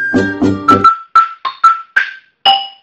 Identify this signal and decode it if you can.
There is no code.